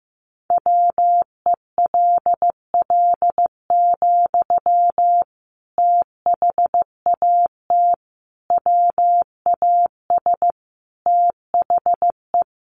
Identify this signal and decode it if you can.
Morse code, heard as a keyed tone, about 15 wpm, 700 Hz. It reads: WELL, THAT WAS THE
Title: Sentences from Top 1200 Words 15wpm